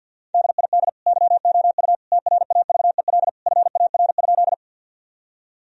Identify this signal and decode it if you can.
Morse code, heard as a keyed tone, about 53 wpm, 700 Hz. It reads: DID YOU TRAVEL FAR?
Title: Sentences from Top 1000 Words - Review Part 2 - 50wpm